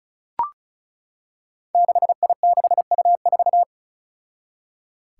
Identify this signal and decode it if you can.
Morse code, heard as a keyed tone, about 36 wpm, 700 Hz. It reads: 6I6U4
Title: Five Challenging Letter-Number Combinations 35wpm